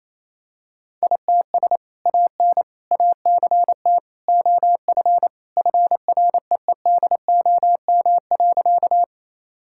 Morse code, heard as a keyed tone, about 28 wpm, 700 Hz. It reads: ITS AN ACT OF FREEDOM.
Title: Sentences from Top 1000 Words - Review Part 5 - 28wpm